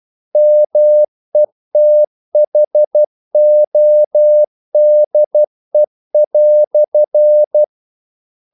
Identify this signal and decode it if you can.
Morse code, heard as a keyed tone, about 12 wpm, 600 Hz. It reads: METHODE"
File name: CW / Practice / EnglishQuotations0112